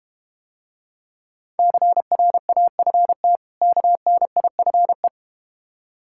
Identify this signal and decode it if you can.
Morse code, heard as a keyed tone, about 32 wpm, 700 Hz. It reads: CRAFT KNIFE